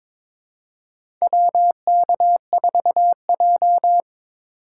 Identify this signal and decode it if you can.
Morse code, heard as a keyed tone, about 22 wpm, 700 Hz. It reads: WK4J
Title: USA Call Signs - 4x Character Spacing - 22wpm